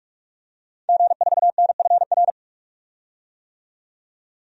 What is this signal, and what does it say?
Morse code, heard as a keyed tone, about 46 wpm, 700 Hz. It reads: G4NFR
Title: DXCC-100 Call Signs - 5 Characters - 45wpm